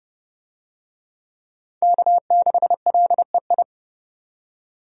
Morse code, heard as a keyed tone, about 30 wpm, 700 Hz. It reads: K6LEI
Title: USA Call Signs - 5 Characters - 30wpm